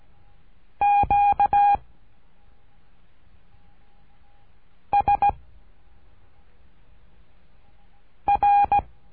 Morse code, heard as a keyed tone, about 17 wpm, 810 Hz. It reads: Q S R
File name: cursos / CURSO DE TELEGRAFIA / Telegrafia - Trilha 13